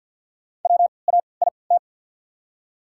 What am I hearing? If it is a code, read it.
Morse code, heard as a keyed tone, about 54 wpm, spaced as for 29 wpm, 690 Hz. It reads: WAIT
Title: 3x Character Spacing - Top 500 Words - 50wpm